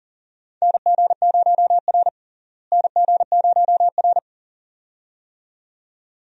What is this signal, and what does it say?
Morse code, heard as a keyed tone, about 40 wpm, 700 Hz. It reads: NG0R NG0R
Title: USA Call Signs - 4 Characters - Repeated 2x - 40wpm